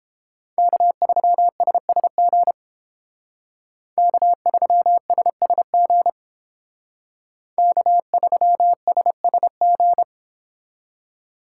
Morse code, heard as a keyed tone, about 31 wpm, 700 Hz. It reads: K3SSG K3SSG K3SSG